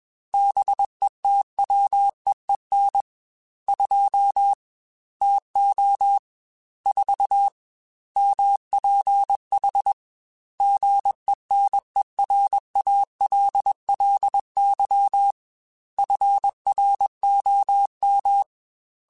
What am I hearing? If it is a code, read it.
Morse code, heard as a keyed tone, about 21 wpm, 800 Hz. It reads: BETWEEN 2 TO 4 MPH GENERALLY FROM
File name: Weather Sets / Weather Sets-mp3s / Weather 4 14 @20